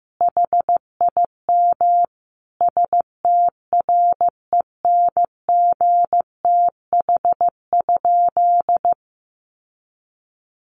Morse code, heard as a keyed tone, about 15 wpm, 700 Hz. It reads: HIM STRENGTH?
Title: Sentences from Top 1300 Words 15wpm